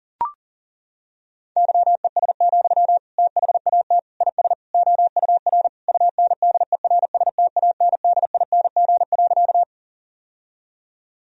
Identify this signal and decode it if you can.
Morse code, heard as a keyed tone, about 40 wpm, 700 Hz. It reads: YES, THAT IS OUR UNDERSTANDING.
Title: Sentences from Top 1000 Words - Review Part 3 - 40wpm